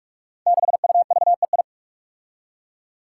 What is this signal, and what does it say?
Morse code, heard as a keyed tone, about 44 wpm, 700 Hz. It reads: 6UVEI
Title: Five Challenging Letter-Number Combinations 45wpm